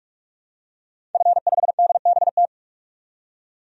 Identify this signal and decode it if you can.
Morse code, heard as a keyed tone, about 46 wpm, 700 Hz. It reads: F5DBT
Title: DXCC-100 Call Signs - 5 Characters - 45wpm